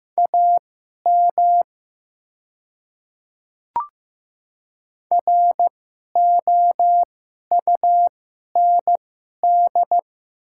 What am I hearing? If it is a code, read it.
Morse code, heard as a keyed tone, about 15 wpm, spaced as for 11 wpm, 700 Hz. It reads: AM ROUND
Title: 2x Character Spacing - Top 500 Words - 15wpm